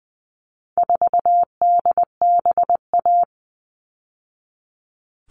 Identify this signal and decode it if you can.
Morse code, heard as a keyed tone, about 20 wpm, 700 Hz. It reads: 4DBA
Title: All Cut Numbers - 4-Digit 20wpm